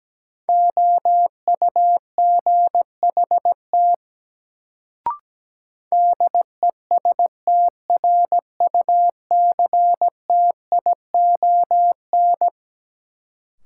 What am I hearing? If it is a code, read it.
Morse code, heard as a keyed tone, about 17 wpm, 700 Hz. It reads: OUGHT DESTRUCTION
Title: Sets of 5 Words Encore - New Word Drill - 17wpm